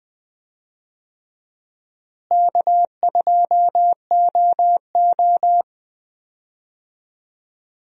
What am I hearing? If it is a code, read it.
Morse code, heard as a keyed tone, about 21 wpm, 700 Hz. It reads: K2OO
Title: USA Call Signs - 4 Characters - Repeated 2x - 20wpm